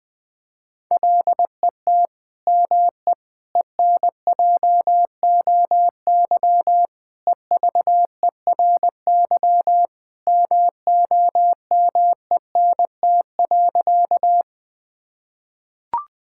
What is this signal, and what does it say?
Morse code, heard as a keyed tone, about 20 wpm, 700 Hz. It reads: LET ME ENJOY EVERY MOMENT.
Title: Sentences from Top 1000 Words - Review Part 5 - 20wpm